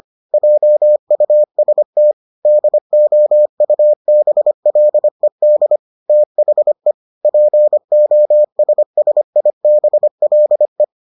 Morse code, heard as a keyed tone, about 25 wpm, 600 Hz. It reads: JUST DOUBLED THE POSSIBLE